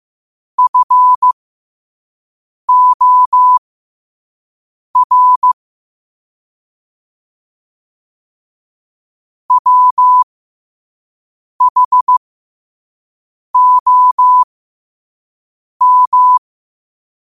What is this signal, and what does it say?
Morse code, heard as a keyed tone, about 15 wpm, spaced as for 5 wpm, 1000 Hz. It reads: FOR WHOM
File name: qotdfiles / 5wpm